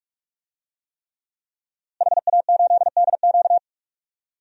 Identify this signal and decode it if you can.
Morse code, heard as a keyed tone, about 46 wpm, 700 Hz. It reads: HA8DQ
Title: DXCC-100 Call Signs - 5 Characters - Repeated 3x - 45wpm